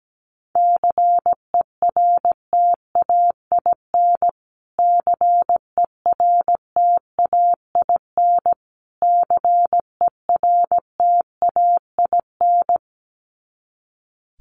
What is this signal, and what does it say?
Morse code, heard as a keyed tone, about 17 wpm, 700 Hz. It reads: CERTAIN CERTAIN CERTAIN
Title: Top 500 Words Repeated 17wpm